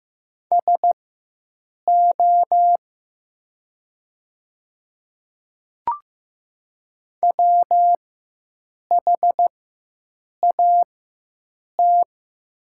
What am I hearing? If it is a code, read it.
Morse code, heard as a keyed tone, about 15 wpm, spaced as for 7 wpm, 700 Hz. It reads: SO WHAT
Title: 4x Character Spacing - Top 300 Words - 15wpm